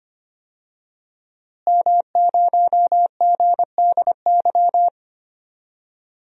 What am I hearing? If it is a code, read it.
Morse code, heard as a keyed tone, about 25 wpm, 700 Hz. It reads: M0GDY
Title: DXCC-100 Call Signs - 5 Characters - Repeated 2x - 25wpm